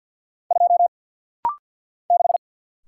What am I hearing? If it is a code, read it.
Morse code, heard as a keyed tone, about 50 wpm, 700 Hz. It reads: 2 6
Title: Single Letter-Number - Rapid-Fire 50wpm